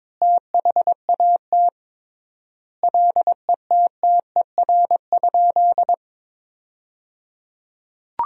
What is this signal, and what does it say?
Morse code, heard as a keyed tone, about 22 wpm, 700 Hz. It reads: THAT LETTER?
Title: 3x Word-Spacing - Sentences from Top 300 Words - 22wpm